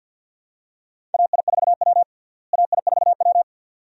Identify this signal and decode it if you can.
Morse code, heard as a keyed tone, about 46 wpm, 700 Hz. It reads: AI4W AI4W
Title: USA Call Signs - 4 Characters - Repeated 2x - 50wpm